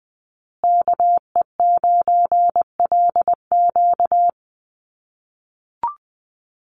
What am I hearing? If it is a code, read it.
Morse code, heard as a keyed tone, about 20 wpm, 700 Hz. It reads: KE9LQ
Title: USA Call Signs Repeated 3x 20wpm